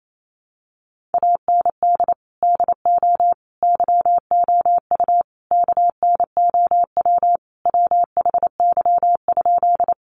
Morse code, heard as a keyed tone, about 28 wpm, 700 Hz. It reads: AND DO YOU KNOW WHY?
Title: Sentences from Top 200 Words 28wpm